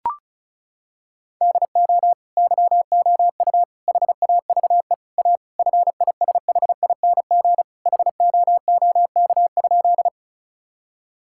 Morse code, heard as a keyed tone, about 34 wpm, 700 Hz. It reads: DO YOU HAVE A FISHING HOOK?
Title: Sentences from Top 1600 Words 35wpm